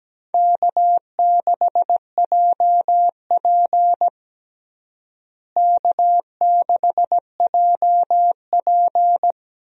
Morse code, heard as a keyed tone, about 17 wpm, 700 Hz. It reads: K6JP K6JP
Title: USA Call Signs - 4 Characters - Repeated 3x - 17wpm